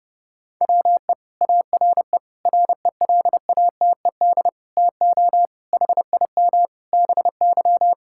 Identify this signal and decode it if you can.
Morse code, heard as a keyed tone, about 30 wpm, 700 Hz. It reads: WE ARE RELATED TO HIM BY
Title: Sentences from Top 1000 Words - Review Part 5 - 30wpm